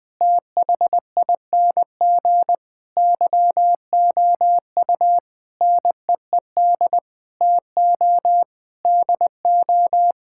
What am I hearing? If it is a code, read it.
Morse code, heard as a keyed tone, about 20 wpm, 700 Hz. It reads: THING YOU NEED TO DO
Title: Sets of 8 Words - 20wpm